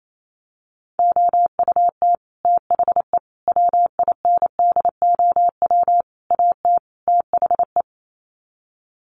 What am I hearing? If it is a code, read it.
Morse code, heard as a keyed tone, about 28 wpm, 700 Hz. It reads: OUT THE WINDOW AT THE
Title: Sets of 5 Words Encore2 28wpm